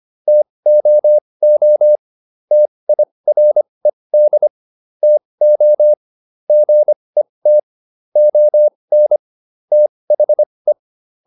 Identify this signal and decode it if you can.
Morse code, heard as a keyed tone, about 25 wpm, spaced as for 20 wpm, 600 Hz. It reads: TOO TIRED TO GET ON THE